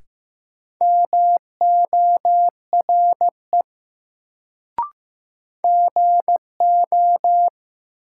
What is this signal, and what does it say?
Morse code, heard as a keyed tone, about 15 wpm, 700 Hz. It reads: MORE GO
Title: Sentences from Top 100 Words - New Word Drill - 15wpm